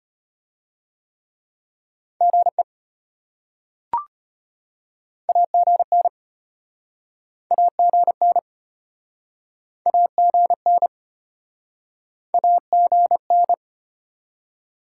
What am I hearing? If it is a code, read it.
Morse code, heard as a keyed tone, about 33 wpm, 700 Hz. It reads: GE AGN AGN AGN AGN